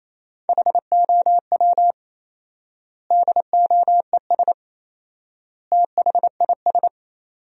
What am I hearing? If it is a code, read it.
Morse code, heard as a keyed tone, about 28 wpm, 700 Hz. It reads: HOW DOES THIS
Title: 4x Word-Spacing - Sentences from Top 400 Words - 28wpm